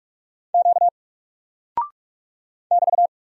Morse code, heard as a keyed tone, about 46 wpm, 700 Hz. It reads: Q <BT>
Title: Single Character (Letter:Number:Punctuation:Prosign) 45wpm